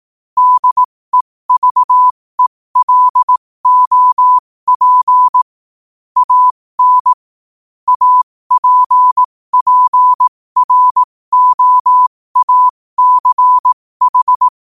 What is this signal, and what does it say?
Morse code, heard as a keyed tone, about 18 wpm, spaced as for 15 wpm, 1000 Hz. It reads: DEVELOP AN APPROACH
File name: qotdfiles / 15wpm